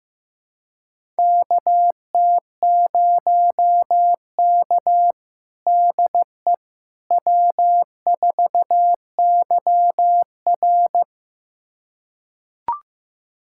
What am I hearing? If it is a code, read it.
Morse code, heard as a keyed tone, about 15 wpm, 700 Hz. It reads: KT0K DE W4YR
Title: USA Callsign DE Callsign - 4 Characters - 15wpm